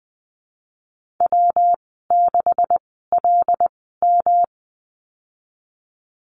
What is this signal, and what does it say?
Morse code, heard as a keyed tone, about 19 wpm, spaced as for 14 wpm, 700 Hz. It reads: W6LM
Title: 161 Lesson 26g - Letter g - Call Signs 20wpm